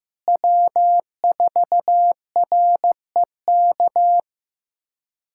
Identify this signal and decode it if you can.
Morse code, heard as a keyed tone, about 15 wpm, 700 Hz. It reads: W4REK
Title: USA Callsign DE Callsign 15wpm